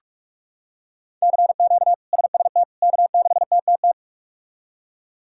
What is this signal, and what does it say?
Morse code, heard as a keyed tone, about 41 wpm, 700 Hz. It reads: CQ SST K6TTT
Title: K1USN SST QSO Practice - 45wpm